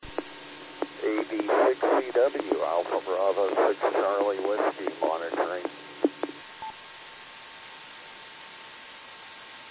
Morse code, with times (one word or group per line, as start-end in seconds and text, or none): none